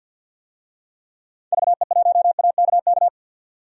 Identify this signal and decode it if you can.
Morse code, heard as a keyed tone, about 51 wpm, 700 Hz. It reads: VE1AKK
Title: DXCC-100 Call Signs - 6 Characters - 50wpm